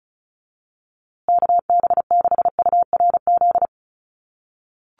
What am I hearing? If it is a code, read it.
Morse code, heard as a keyed tone, about 34 wpm, 700 Hz. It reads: KB6URZ
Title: USA Call Signs Repeated 3x 35wpm